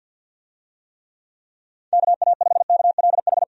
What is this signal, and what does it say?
Morse code, heard as a keyed tone, about 48 wpm, 700 Hz. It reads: KA5KLH
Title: USA Call Signs - 6 Characters - 50wpm